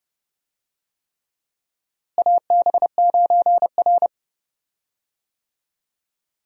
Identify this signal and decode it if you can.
Morse code, heard as a keyed tone, about 30 wpm, 700 Hz. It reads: AB9R
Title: USA Call Signs - 4 Characters - Repeated 3x - 30wpm